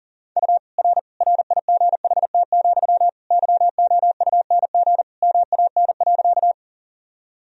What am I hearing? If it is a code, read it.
Morse code, heard as a keyed tone, about 40 wpm, 700 Hz. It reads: U R RIGHT, YOUNG MAN.
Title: Sentences from Top 400 Words and Abbreviations 40wpm